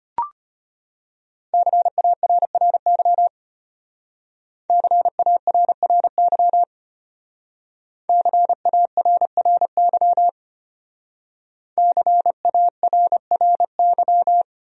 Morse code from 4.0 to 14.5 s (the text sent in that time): CARRY CARRY CARRY